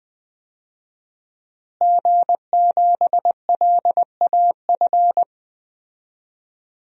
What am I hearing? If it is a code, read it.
Morse code, heard as a keyed tone, about 20 wpm, 700 Hz. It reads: G7LAF
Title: DXCC-100 Call Signs - 5 Characters - Repeated 2x - 20wpm